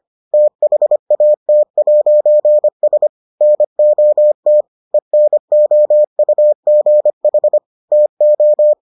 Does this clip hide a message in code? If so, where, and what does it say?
Yes: Morse code from 0.3 to 8.7 s, heard as a keyed tone, about 25 wpm, 600 Hz. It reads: THAT'S NOT ENOUGH TO